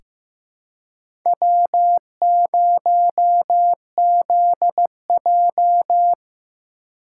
Morse code, heard as a keyed tone, about 15 wpm, 700 Hz. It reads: W0ZJ
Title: USA Call Signs - 4 Characters - 4x Character Spacing - 15wpm